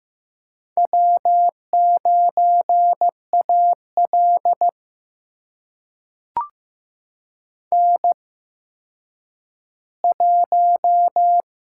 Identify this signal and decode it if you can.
Morse code, heard as a keyed tone, about 15 wpm, spaced as for 9 wpm, 700 Hz. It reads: W9AL N1
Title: USA Call Signs - 8x Character Spacing - 15wpm